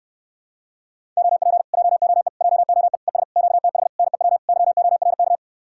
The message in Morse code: OP JOE JOE ES QTH NEW YORK

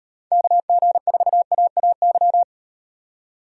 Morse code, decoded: KG4AAY

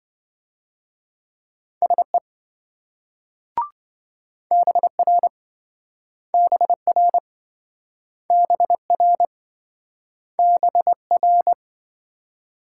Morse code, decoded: SE BR BR BR BR

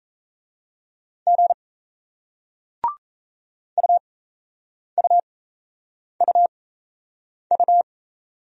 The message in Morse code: G UUUU

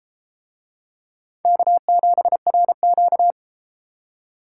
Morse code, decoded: K7RQ

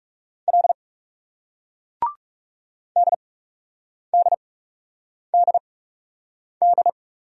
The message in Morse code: L DDDD

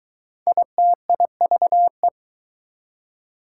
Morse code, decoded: ITIVE

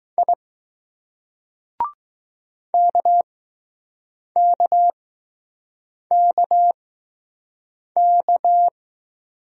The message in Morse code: I KKKK

